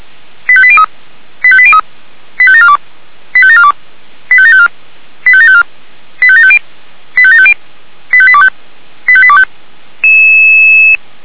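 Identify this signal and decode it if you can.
Five-tone selective call (ZVEI), heard as quick runs of five tones, tones of about 70 ms. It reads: 85803 85803 85742 85742 85754 85754 85750 85750 85825 85825
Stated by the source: ZVEI five-tone selective call 85726 received with an SDR.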